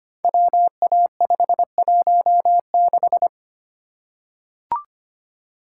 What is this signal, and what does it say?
Morse code, heard as a keyed tone, about 25 wpm, 700 Hz. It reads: WA516
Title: Code Groups - 5 Letter:Numbers - 25wpm